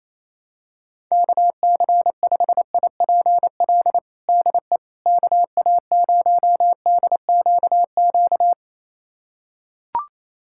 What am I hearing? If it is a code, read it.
Morse code, heard as a keyed tone, about 28 wpm, 700 Hz. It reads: KC5IPL DE KA0DQQ
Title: USA Callsign DE Callsign - 6 Characters - 28wpm